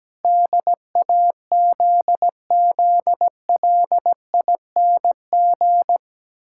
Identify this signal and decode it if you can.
Morse code, heard as a keyed tone, about 17 wpm, 700 Hz. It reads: DAZZLING
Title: Pangram Challenge v2 - 17wpm